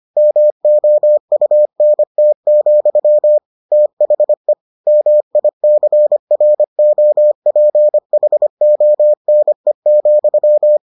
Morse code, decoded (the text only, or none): MOUNT, THE MICROPHONE,